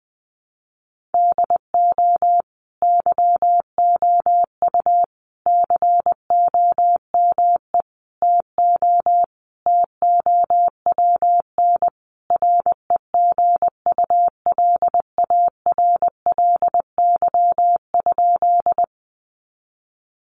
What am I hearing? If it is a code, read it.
Morse code, heard as a keyed tone, about 19 wpm, 700 Hz. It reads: DO YOU COME TO TOWN REGULARLY?
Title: Sentences from Top 1400 Words 20wpm